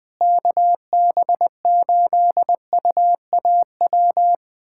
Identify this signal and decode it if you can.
Morse code, heard as a keyed tone, about 20 wpm, 700 Hz. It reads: KB8UAW
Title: USA Callsign DE Callsign 20wpm